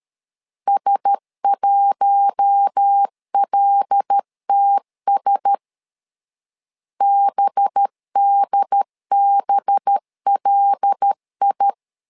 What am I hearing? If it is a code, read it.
Morse code, heard as a keyed tone, about 13 wpm, 800 Hz. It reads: S1LTS BDBLI
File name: LEZIONI / 5 LBVC / 5 LB 12wpm 1906